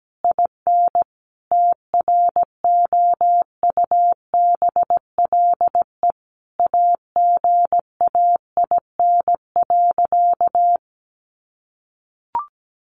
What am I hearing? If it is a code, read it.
Morse code, heard as a keyed tone, about 17 wpm, 700 Hz. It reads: IN TROUBLE AGAIN.
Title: Sentences from Top 1200 Words 17wpm